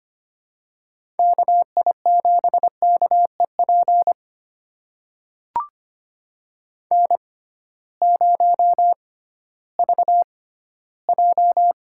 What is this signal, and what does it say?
Morse code, heard as a keyed tone, about 25 wpm, spaced as for 15 wpm, 700 Hz. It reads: KI7KEP N0VJ
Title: USA Call Signs - 6x Character Spacing - 25wpm